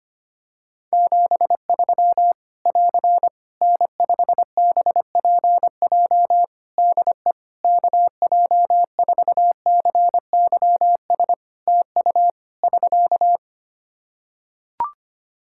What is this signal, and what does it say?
Morse code, heard as a keyed tone, about 25 wpm, 700 Hz. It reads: 73 <AR> N5BPJ DE KJ4CYS TU <SK>
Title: Practice QSO - Exchange 4 - Ending - 25wpm